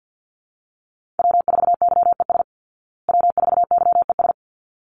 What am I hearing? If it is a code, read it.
Morse code, heard as a keyed tone, about 50 wpm, 700 Hz. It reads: W4YES W4YES
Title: Call Signs Repeated 3x - 50wpm